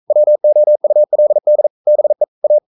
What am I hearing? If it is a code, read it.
Morse code, heard as a keyed tone, about 42 wpm, 600 Hz. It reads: WOULD BE A
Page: Romanen Never af Ken Follet på engelsk.